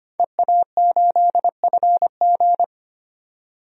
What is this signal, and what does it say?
Morse code, heard as a keyed tone, about 25 wpm, 700 Hz. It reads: EA8FG
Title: DXCC-100 Call Signs - 5 Characters - 25wpm